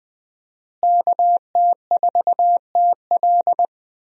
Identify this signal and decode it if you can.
Morse code, heard as a keyed tone, about 20 wpm, 700 Hz. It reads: KT4TL
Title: USA Call Signs - 5 Characters - Repeated 2x - 20wpm